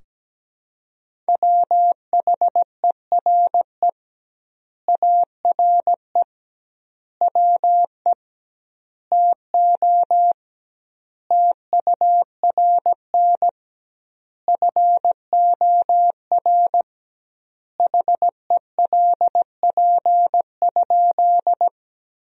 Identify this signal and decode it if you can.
Morse code, heard as a keyed tone, about 17 wpm, 700 Hz. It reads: WHERE ARE WE TO TURN FOR HELP?
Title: 2x Word-Spacing - Sentences from Top 200 Words - 17wpm